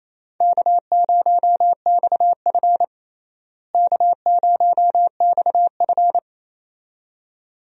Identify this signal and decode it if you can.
Morse code, heard as a keyed tone, about 28 wpm, 700 Hz. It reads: K0XF K0XF
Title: USA Call Signs - 4 Characters - Repeated 2x - 28wpm